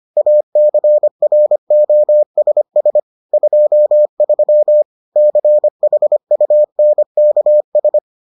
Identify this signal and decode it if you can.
Morse code, heard as a keyed tone, about 25 wpm, 600 Hz. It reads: ACROSS 23 CHUNKS